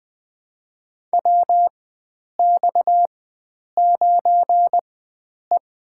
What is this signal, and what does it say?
Morse code, heard as a keyed tone, about 20 wpm, spaced as for 9 wpm, 700 Hz. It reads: WX9E
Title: USA Call Signs - 4 Characters - 4x Character Spacing - 20wpm